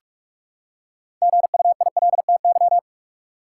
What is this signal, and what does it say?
Morse code, heard as a keyed tone, about 47 wpm, 700 Hz. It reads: GUILTY
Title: Six Letter Words - 45wpm